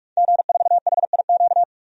Morse code, heard as a keyed tone, about 41 wpm, 700 Hz. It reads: G4HIQ
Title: DXCC-100 Call Signs - 5 Characters - 45wpm